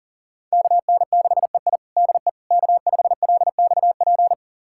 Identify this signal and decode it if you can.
Morse code, heard as a keyed tone, about 40 wpm, 700 Hz. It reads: KN6EI DE K5LXP
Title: USA Callsign DE Callsign - 5 Characters - 40wpm